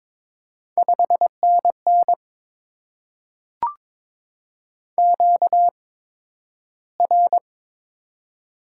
Morse code, heard as a keyed tone, about 22 wpm, spaced as for 12 wpm, 700 Hz. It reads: 5NN QR
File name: Top 90 QSO Elements - 8x Character Spacing - 22wpm